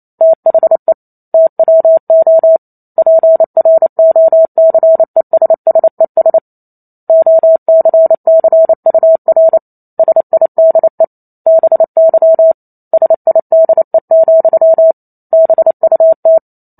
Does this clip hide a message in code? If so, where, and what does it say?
Yes: Morse code from 0.2 to 16.4 s, heard as a keyed tone, about 28 wpm, 650 Hz. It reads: THE TWO PROCESSES OCCUR SIDE BY SIDE, BUT